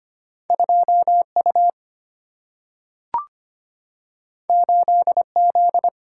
Morse code, heard as a keyed tone, about 25 wpm, 700 Hz. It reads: 2U 8Z
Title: Single-Letter Single-Number 25wpm